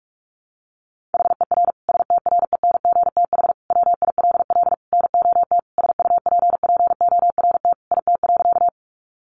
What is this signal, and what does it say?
Morse code, heard as a keyed tone, about 44 wpm, 700 Hz. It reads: HER STRENGTH WILL NOT SUPPORT IT.
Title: Sentences from Top 1300 Words 45wpm